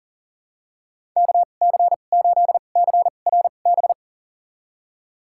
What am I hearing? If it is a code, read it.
Morse code, heard as a keyed tone, about 40 wpm, spaced as for 29 wpm, 700 Hz. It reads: KC8CRB
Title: USA Call Signs - 6 Characters - 2x Character Spacing - 40wpm